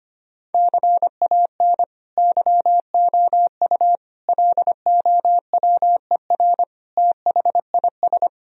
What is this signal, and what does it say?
Morse code, heard as a keyed tone, about 25 wpm, 700 Hz. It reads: CAN YOU LOWER THIS
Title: Sentences from Top 1000 Words - Review Part 4 - 25wpm